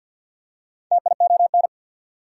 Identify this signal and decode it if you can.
Morse code, heard as a keyed tone, about 48 wpm, 690 Hz. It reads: TION
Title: Prefixes and Suffixes 50wpm